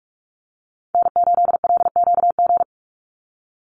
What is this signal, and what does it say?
Morse code, heard as a keyed tone, about 44 wpm, 700 Hz. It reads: N8LQG
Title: USA Call Signs 45wpm